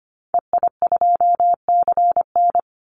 Morse code, heard as a keyed tone, about 25 wpm, 700 Hz. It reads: EI2CN
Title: Call Signs - 25wpm